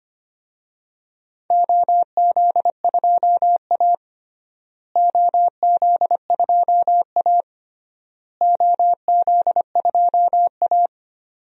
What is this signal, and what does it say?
Morse code, heard as a keyed tone, about 25 wpm, 700 Hz. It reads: OZ2A OZ2A OZ2A
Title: DXCC-100 Call Signs - 4 Characters - Repeated 3x - 25wpm